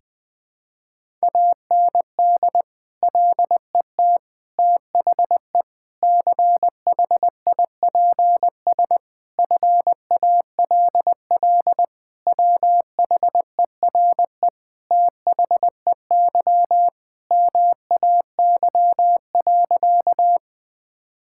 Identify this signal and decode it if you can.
Morse code, heard as a keyed tone, about 21 wpm, 700 Hz. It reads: AND LET THE CHIPS FALL WHERE THEY MAY.
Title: Sentences with 8-Word Phrases - Spoken Beginning - 20wpm